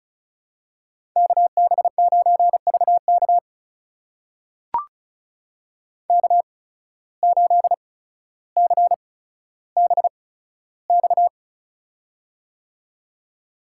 Morse code, heard as a keyed tone, about 36 wpm, spaced as for 15 wpm, 700 Hz. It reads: KB9VK K8CBX
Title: USA Call Signs - 5 Characters - 8x Character Spacing - 35wpm